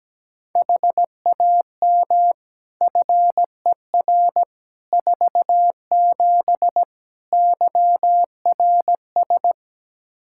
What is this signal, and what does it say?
Morse code, heard as a keyed tone, about 17 wpm, 700 Hz. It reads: HAM FER 47 YRS
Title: K1USN SST QSO Practice 17wpm